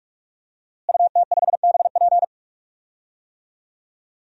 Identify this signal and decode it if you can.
Morse code, heard as a keyed tone, about 42 wpm, 700 Hz. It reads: UT5BP
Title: DXCC-100 Call Signs - 5 Characters - Repeated 3x - 45wpm